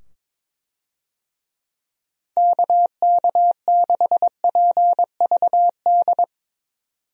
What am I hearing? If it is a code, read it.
Morse code, heard as a keyed tone, about 22 wpm, 700 Hz. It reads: KK6PVD
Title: USA Call Signs - 6 Characters - 22wpm